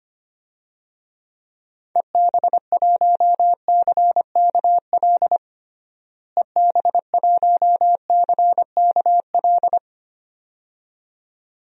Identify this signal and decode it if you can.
Morse code, heard as a keyed tone, about 25 wpm, 700 Hz. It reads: EB1CKL EB1CKL
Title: DXCC-100 Call Signs - 6 Characters - Repeated 2x - 25wpm